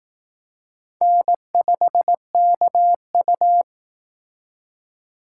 Morse code, heard as a keyed tone, about 18 wpm, 700 Hz. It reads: N5KU
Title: Speed Racing - USA Call Signs 15wpm